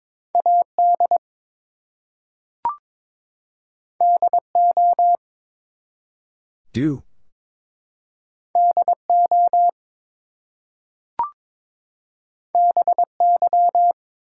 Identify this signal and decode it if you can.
Morse code, heard as a keyed tone, about 22 wpm, 700 Hz. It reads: AD DO DO BY
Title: Two Letter Words - 22wpm